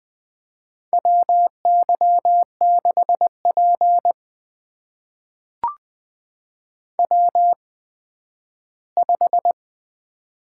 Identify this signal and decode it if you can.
Morse code, heard as a keyed tone, about 20 wpm, spaced as for 12 wpm, 700 Hz. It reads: WY6P W5
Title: USA Call Signs - 8x Character Spacing - 20wpm